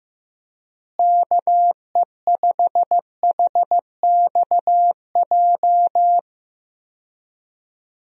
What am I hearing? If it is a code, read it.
Morse code, heard as a keyed tone, about 15 wpm, 700 Hz. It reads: KE5HXJ
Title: USA Call Signs - 6 Characters - Repeated 2x - 15wpm